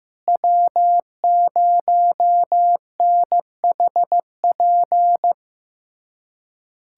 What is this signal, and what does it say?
Morse code, heard as a keyed tone, about 15 wpm, 700 Hz. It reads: W0NHP
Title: USA Callsign DE Callsign - 5 Characters - 15wpm